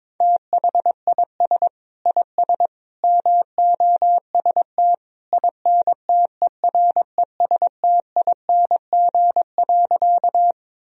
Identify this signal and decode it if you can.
Morse code, heard as a keyed tone, about 22 wpm, 700 Hz. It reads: THIS IS MOST INTERESTING.
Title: Sentences from Top 1000 Words - Review Part 1 - 22wpm